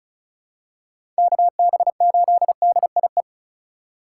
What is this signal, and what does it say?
Morse code, heard as a keyed tone, about 36 wpm, 700 Hz. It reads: KB8DIE
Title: USA Call Signs - 4x Character Spacing - 35wpm